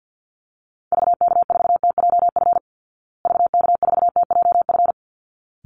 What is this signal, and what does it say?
Morse code, heard as a keyed tone, about 48 wpm, 700 Hz. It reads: VK4TJF VK4TJF
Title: Call Signs Repeated 3x - 50wpm